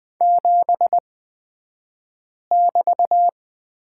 Morse code, heard as a keyed tone, about 20 wpm, spaced as for 5 wpm, 700 Hz. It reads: 7<BT>
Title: 242 Lesson 39d - Letter z - Two Characters 20wpm